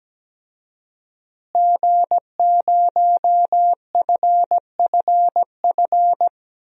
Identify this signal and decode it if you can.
Morse code, heard as a keyed tone, about 17 wpm, 700 Hz. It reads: G0FFF
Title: DXCC-100 Call Signs - 5 Characters - Repeated 3x - 17wpm